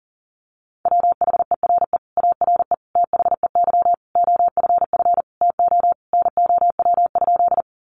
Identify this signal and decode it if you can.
Morse code, heard as a keyed tone, about 40 wpm, 700 Hz. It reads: WHERE ARE THEY OFF TO NOW?
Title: Sentences from Top 200 Words 40wpm